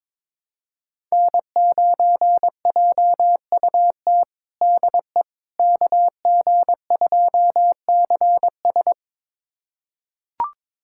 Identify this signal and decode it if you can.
Morse code, heard as a keyed tone, about 22 wpm, 700 Hz. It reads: N9JUT DE KG2CS
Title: USA Callsign DE Callsign - 5 Characters - 22wpm